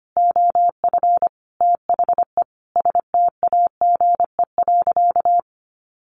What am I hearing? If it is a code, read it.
Morse code, heard as a keyed tone, about 25 wpm, 700 Hz. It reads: OF THE STAGE.
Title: Sentences from Top 1400 Words 25wpm